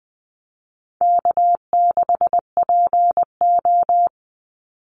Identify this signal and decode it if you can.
Morse code, heard as a keyed tone, about 20 wpm, 700 Hz. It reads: K6PO
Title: Call Signs Repeated 3x - 20wpm